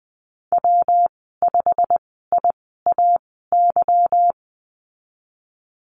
Morse code, heard as a keyed tone, about 20 wpm, spaced as for 15 wpm, 700 Hz. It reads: W5IAY
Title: 193 Lesson 31g - Letter v - Call Signs 20wpm